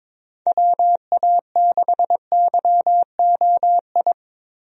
Morse code, heard as a keyed tone, about 22 wpm, 700 Hz. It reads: WA6YOI
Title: USA Call Signs - 6 Characters - Repeated 3x - 22wpm